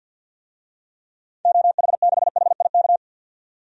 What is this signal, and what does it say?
Morse code, heard as a keyed tone, about 51 wpm, 700 Hz. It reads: OH6HIK
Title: DXCC-100 Call Signs - 6 Characters - 50wpm